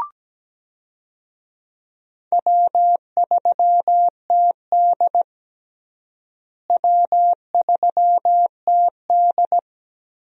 Morse code, heard as a keyed tone, about 17 wpm, 700 Hz. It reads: W3TD W3TD
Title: USA Call Signs - 4 Characters - Repeated 3x - 17wpm